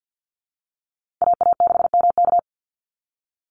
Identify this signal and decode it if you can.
Morse code, heard as a keyed tone, about 48 wpm, 700 Hz. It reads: AA6MK
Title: Call Signs Repeated 2x - 50wpm